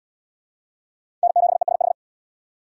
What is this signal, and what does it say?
Morse code, heard as a keyed tone, about 72 wpm, 700 Hz. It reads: N7ESH